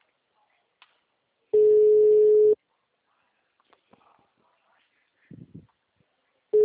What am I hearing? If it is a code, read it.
Ringback tone (Europe, 425 Hz).